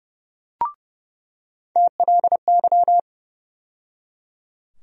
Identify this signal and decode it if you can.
Morse code, heard as a keyed tone, about 30 wpm, 700 Hz. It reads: TLY
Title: USA Call Sign Postfixes - 3 Letters 30wpm